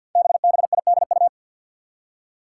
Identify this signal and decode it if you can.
Morse code, heard as a keyed tone, about 47 wpm, 700 Hz. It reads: BBIDU
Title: Five Challenging Letter-Number Combinations 50wpm